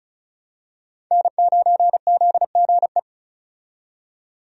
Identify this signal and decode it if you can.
Morse code, heard as a keyed tone, about 34 wpm, 700 Hz. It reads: N9ZGE
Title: USA Call Signs - 5 Characters - 35wpm